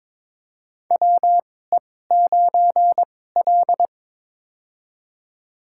Morse code, heard as a keyed tone, about 22 wpm, spaced as for 16 wpm, 700 Hz. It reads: WE9L
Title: USA Call Signs - 4 Characters - 2x Character Spacing - 22wpm